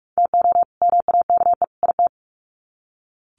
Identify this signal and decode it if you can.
Morse code, heard as a keyed tone, about 44 wpm, 700 Hz. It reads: TO MAKE IT
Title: Sets of 3 Words 45wpm